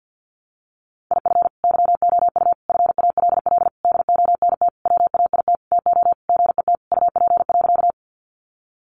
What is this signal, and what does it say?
Morse code, heard as a keyed tone, about 50 wpm, 700 Hz. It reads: IF YOU FALL DONT WAIT TO GET UP.
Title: Sentences from Top 500 Words 50wpm